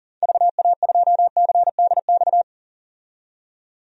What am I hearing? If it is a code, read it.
Morse code, heard as a keyed tone, about 38 wpm, 700 Hz. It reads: VA2CDX
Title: DXCC-100 Call Signs - 6 Characters - 40wpm